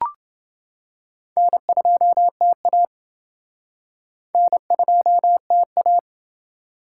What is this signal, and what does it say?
Morse code, heard as a keyed tone, about 29 wpm, 700 Hz. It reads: N2TA N2TA